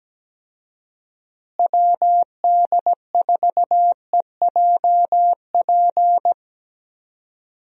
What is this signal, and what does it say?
Morse code, heard as a keyed tone, about 17 wpm, 700 Hz. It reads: WD4EJP
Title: USA Call Signs - 6 Characters - Repeated 3x - 17wpm